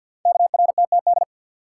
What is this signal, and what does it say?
Morse code, heard as a keyed tone, about 51 wpm, 700 Hz. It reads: KRTTD
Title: Code Groups - 5 Letter:Numbers - 50wpm